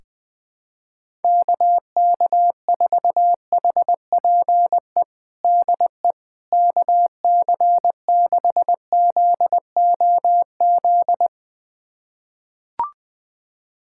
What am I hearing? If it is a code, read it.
Morse code, heard as a keyed tone, about 21 wpm, 700 Hz. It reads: KK4HPE DE KC6ZOZ
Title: USA Callsign DE Callsign - 6 Characters - 20wpm